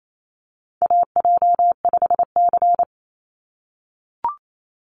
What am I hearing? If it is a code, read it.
Morse code, heard as a keyed tone, about 27 wpm, 700 Hz. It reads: AJ5C
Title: NA SOTA Chaser Callsigns 28wpm